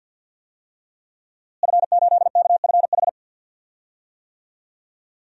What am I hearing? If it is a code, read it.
Morse code, heard as a keyed tone, about 49 wpm, 700 Hz. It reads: F8KFH
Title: DXCC-100 Call Signs - 5 Characters - Repeated 2x - 50wpm